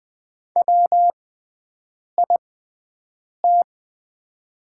Morse code, heard as a keyed tone, about 20 wpm, spaced as for 7 wpm, 700 Hz. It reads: WIT
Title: 6x Character Spacing - Top 100 Words - 20wpm